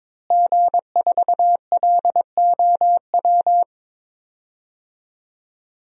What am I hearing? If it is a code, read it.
Morse code, heard as a keyed tone, about 21 wpm, 700 Hz. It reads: G4LOW
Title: DXCC-100 Call Signs - 5 Characters - Repeated 2x - 22wpm